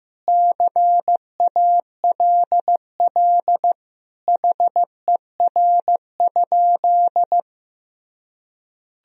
Morse code, heard as a keyed tone, about 15 wpm, 700 Hz. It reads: CALL HER?
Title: Sentences from Top 100 Words and Abbreviations 15wpm